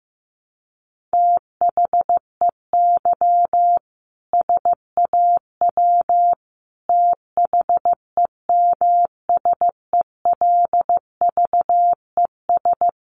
Morse code, heard as a keyed tone, about 15 wpm, 700 Hz. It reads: THEY SAW THEMSELVES
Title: A Sound Of Thunder 15wpm